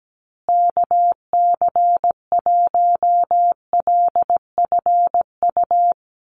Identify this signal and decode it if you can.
Morse code, heard as a keyed tone, about 17 wpm, 700 Hz. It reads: KC1LFU
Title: USA Call Signs Repeated 2x 17wpm